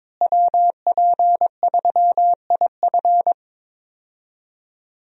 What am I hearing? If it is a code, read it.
Morse code, heard as a keyed tone, about 22 wpm, 700 Hz. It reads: WP3IF
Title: USA Call Signs - 5 Characters - Repeated 2x - 22wpm